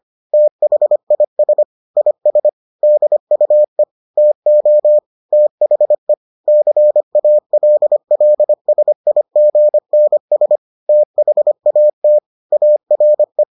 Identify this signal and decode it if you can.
Morse code, heard as a keyed tone, about 25 wpm, 600 Hz. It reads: THIS IS DUE TO THE CALLSIGNS THAT ARE